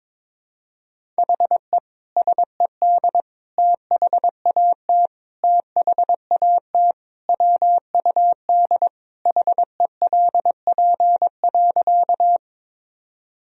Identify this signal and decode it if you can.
Morse code, heard as a keyed tone, about 22 wpm, 700 Hz. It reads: HE SED THAT THAT WUD HELP.
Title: Sentences from Top 200 Words and Abbreviations 22wpm